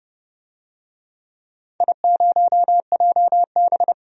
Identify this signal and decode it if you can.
Morse code, heard as a keyed tone, about 31 wpm, 700 Hz. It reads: I0JB
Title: DXCC-100 Call Signs - 4 Characters - 30wpm